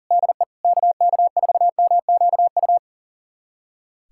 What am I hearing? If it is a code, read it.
Morse code, heard as a keyed tone, about 40 wpm, 700 Hz. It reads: DE KK4MQU
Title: USA Callsign DE Callsign 40wpm